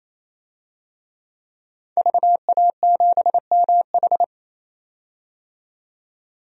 Morse code, heard as a keyed tone, about 28 wpm, 700 Hz. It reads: VA7MH
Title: DXCC-100 Call Signs - 5 Characters - 28wpm